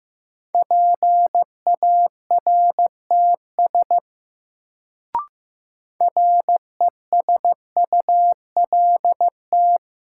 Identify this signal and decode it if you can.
Morse code, heard as a keyed tone, about 15 wpm, 700 Hz. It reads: PARTS RESULT
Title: Sets of 3 Words - New Word Drill - 15wpm